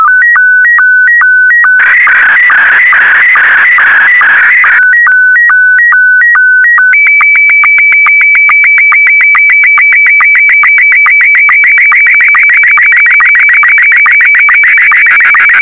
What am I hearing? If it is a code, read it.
SSTV (slow-scan television), Scottie 1.